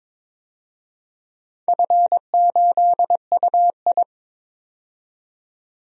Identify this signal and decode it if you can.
Morse code, heard as a keyed tone, about 22 wpm, 690 Hz. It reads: F8UI
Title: DXCC-100 Call Signs - 4 Characters - 22wpm